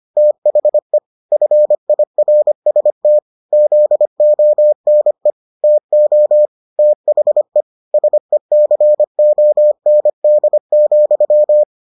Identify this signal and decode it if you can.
Morse code, heard as a keyed tone, about 25 wpm, 600 Hz. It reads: THE FIRST ZONE TO THE SECOND,